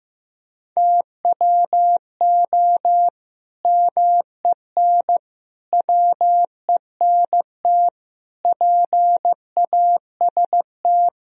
Morse code, heard as a keyed tone, about 15 wpm, 700 Hz. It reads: TWO MEN WENT PAST
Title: Sentences from Top 1000 Words - Review Part 3 - 15wpm